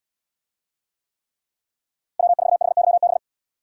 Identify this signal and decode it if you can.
Morse code, heard as a keyed tone, about 72 wpm, 700 Hz. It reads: W4HYB